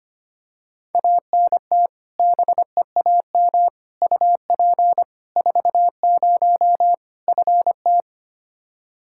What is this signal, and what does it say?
Morse code, heard as a keyed tone, about 25 wpm, 700 Hz. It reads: ANT BEAM UP 40 FT
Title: Practice QSO - Exchange 2 - Info - 25wpm